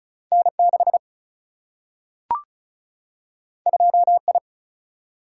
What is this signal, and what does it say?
Morse code, heard as a keyed tone, about 36 wpm, 700 Hz. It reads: N6 2I
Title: Single-Letter Single-Number 35wpm